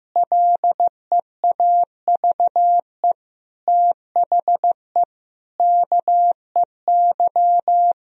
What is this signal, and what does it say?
Morse code, heard as a keyed tone, about 15 wpm, 700 Hz. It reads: LEAVE THE KEY
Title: Sentences from Top 1000 Words - Review Part 4 - 15wpm